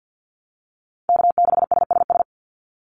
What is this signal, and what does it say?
Morse code, heard as a keyed tone, about 48 wpm, 700 Hz. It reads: K6SSS
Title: Call Signs - 50wpm